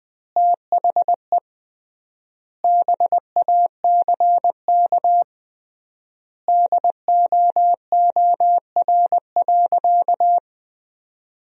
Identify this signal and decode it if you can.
Morse code, heard as a keyed tone, about 20 wpm, 700 Hz. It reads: THE BACK DOOR.
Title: 3x Word-Spacing - Sentences from Top 300 Words - 20wpm